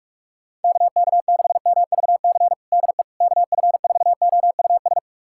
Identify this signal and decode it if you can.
Morse code, heard as a keyed tone, about 45 wpm, 700 Hz. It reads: KK6MVC DE KF4OUS
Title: USA Callsign DE Callsign - 6 Characters - 45wpm